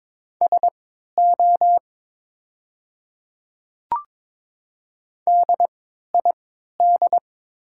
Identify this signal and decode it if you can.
Morse code, heard as a keyed tone, about 22 wpm, spaced as for 13 wpm, 700 Hz. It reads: SO DID
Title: 3x Character Spacing - Top 300 Words - 22wpm